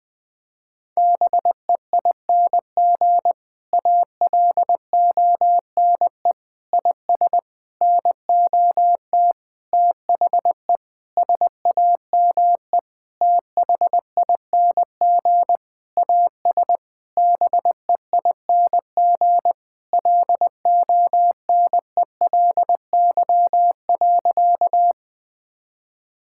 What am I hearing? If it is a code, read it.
Morse code, heard as a keyed tone, about 21 wpm, 700 Hz. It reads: BEING ALONE IS NOT THE SAME THING AS BEING LONELY.
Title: Sentences with 6-Word Phrases - 20wpm